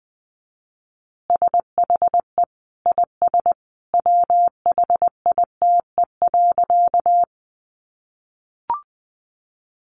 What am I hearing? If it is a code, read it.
Morse code, heard as a keyed tone, about 20 wpm, 700 Hz. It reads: SHE IS WHITE.
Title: Easy Phrases 20wpm